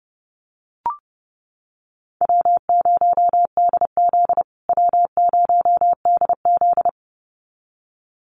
Morse code, heard as a keyed tone, about 30 wpm, 700 Hz. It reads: W0DZ W0DZ
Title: USA Call Signs Repeated 2x 30wpm